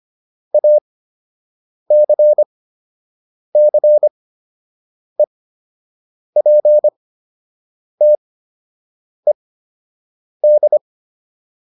Morse code, heard as a keyed tone, about 25 wpm, spaced as for 7 wpm, 600 Hz. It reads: ACCEPTED